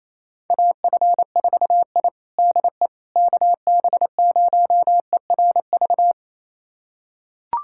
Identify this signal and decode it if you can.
Morse code, heard as a keyed tone, about 28 wpm, 700 Hz. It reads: AF4I DE KB0ERV
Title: USA Callsign DE Callsign 28wpm